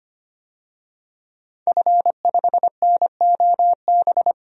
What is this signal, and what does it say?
Morse code, heard as a keyed tone, about 25 wpm, 700 Hz. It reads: F5NOB
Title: DXCC-100 Call Signs - 5 Characters - 25wpm